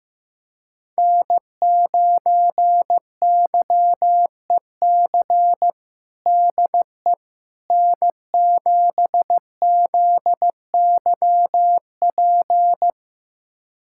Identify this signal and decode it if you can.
Morse code, heard as a keyed tone, about 15 wpm, 700 Hz. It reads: N9YEC DE N7ZYP
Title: USA Callsign DE Callsign - 5 Characters - 15wpm